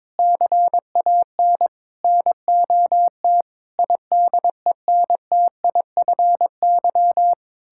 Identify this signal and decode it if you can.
Morse code, heard as a keyed tone, about 22 wpm, 700 Hz. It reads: CAN NOT IDENTIFY
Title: Sentences from Top 1000 Words - Review Part 6 - 22wpm